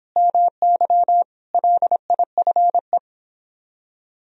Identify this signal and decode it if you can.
Morse code, heard as a keyed tone, about 29 wpm, 700 Hz. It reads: MY LIFE